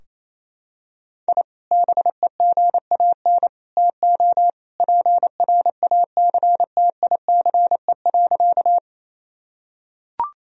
Morse code, heard as a keyed tone, about 28 wpm, 700 Hz. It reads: I BEGAN TO PRACTICE.
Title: Sentences from Top 1000 Words - Review Part 4 - 28wpm